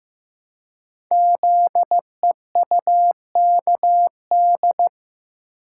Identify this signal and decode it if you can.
Morse code, heard as a keyed tone, about 15 wpm, 700 Hz. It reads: ZEUKD
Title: Code Groups - 5 Letters - 15wpm